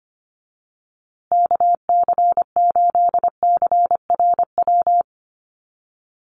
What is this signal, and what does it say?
Morse code, heard as a keyed tone, about 25 wpm, 700 Hz. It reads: KC8CRW
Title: USA Call Signs Repeated 2x 25wpm